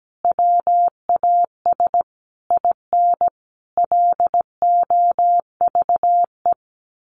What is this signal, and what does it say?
Morse code, heard as a keyed tone, about 17 wpm, 700 Hz. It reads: WAS IN LOVE
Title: Sentences from Top 1300 Words 17wpm